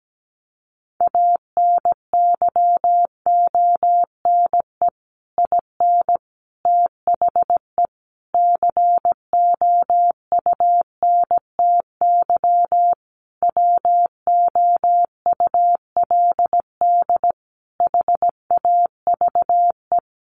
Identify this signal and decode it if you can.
Morse code, heard as a keyed tone, about 17 wpm, 700 Hz. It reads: ANYONE IN THE COUNTY WOULD HAVE
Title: Sentences from Top 1200 Words 17wpm